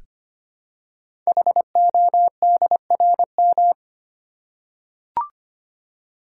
Morse code, heard as a keyed tone, about 25 wpm, 700 Hz. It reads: HODRM
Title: Code Groups - 5 Letters - 25wpm